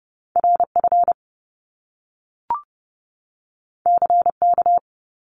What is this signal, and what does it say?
Morse code, heard as a keyed tone, about 30 wpm, 700 Hz. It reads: RF CK
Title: Top Two Letter Combinations in Words 30wpm